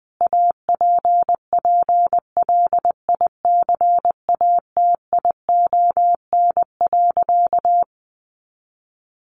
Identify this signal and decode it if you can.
Morse code, heard as a keyed tone, about 20 wpm, 700 Hz. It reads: APPLICATION.
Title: Sentences from Top 1200 Words 20wpm